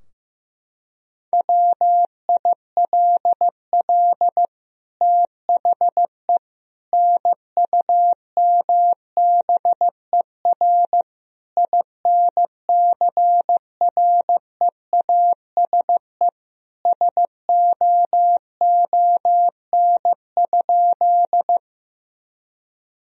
Morse code, heard as a keyed tone, about 16 wpm, 700 Hz. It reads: WILL THE NUMBER INCREASE SOON?
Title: Sentences from Top 1000 Words - Review Part 4 - 15wpm